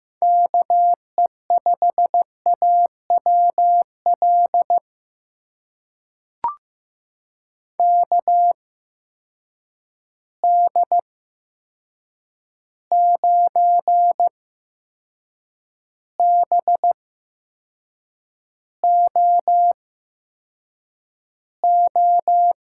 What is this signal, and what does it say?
Morse code, heard as a keyed tone, about 15 wpm, spaced as for 6 wpm, 700 Hz. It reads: KE5AWL KD9BOO
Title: USA Call Signs - 6 Characters - 8x Character Spacing - 15wpm